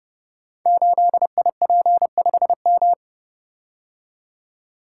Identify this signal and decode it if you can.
Morse code, heard as a keyed tone, about 30 wpm, 700 Hz. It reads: 8IP5M
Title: Code Groups - 5 Letter:Numbers - 30wpm